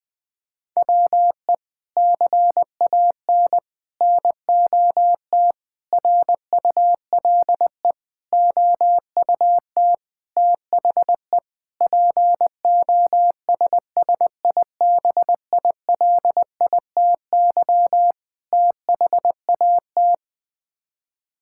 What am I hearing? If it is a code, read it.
Morse code, heard as a keyed tone, about 21 wpm, 700 Hz. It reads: WE CAN NOT RULE OUT THE POSSIBILITY THAT
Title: Sets of 8 Words - 20wpm